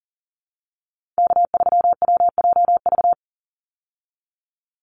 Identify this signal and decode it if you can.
Morse code, heard as a keyed tone, about 38 wpm, 700 Hz. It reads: K3WJV
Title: Call Signs Repeated 3x - 40wpm